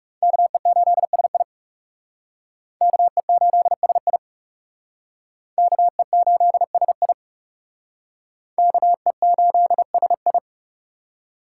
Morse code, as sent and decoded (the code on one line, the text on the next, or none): -.- . ---.. ... .. / -.- . ---.. ... .. / -.- . ---.. ... .. / -.- . ---.. ... ..
KE8SI KE8SI KE8SI KE8SI